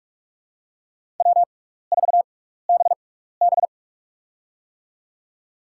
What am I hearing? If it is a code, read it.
Morse code, heard as a keyed tone, about 46 wpm, spaced as for 16 wpm, 700 Hz. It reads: W4BB
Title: USA Call Signs - 4 Characters - 6x Character Spacing - 45wpm